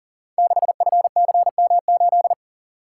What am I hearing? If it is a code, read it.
Morse code, heard as a keyed tone, about 38 wpm, 700 Hz. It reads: 6FCM8
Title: Code Groups - 5 Letter:Numbers - 40wpm